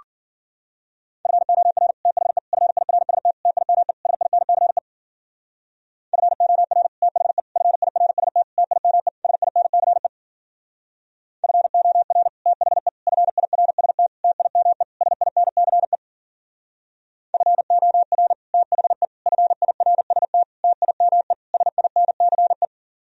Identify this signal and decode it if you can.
Morse code, heard as a keyed tone, about 51 wpm, 700 Hz. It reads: FOR THE FIRST TIME SINCE FOR THE FIRST TIME SINCE FOR THE FIRST TIME SINCE FOR THE FIRST TIME SINCE